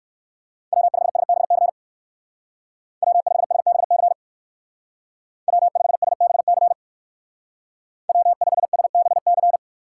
Morse code, heard as a keyed tone, about 59 wpm, 700 Hz. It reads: W5SBC W5SBC W5SBC W5SBC